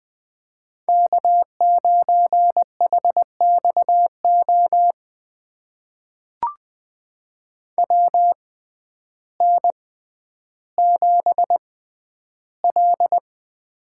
Morse code, heard as a keyed tone, about 20 wpm, spaced as for 11 wpm, 700 Hz. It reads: K9HXO WN7L
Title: USA Call Signs - 6x Character Spacing - 20wpm